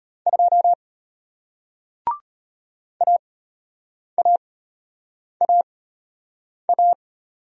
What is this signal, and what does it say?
Morse code, heard as a keyed tone, about 33 wpm, spaced as for 8 wpm, 700 Hz. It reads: 2 AAAA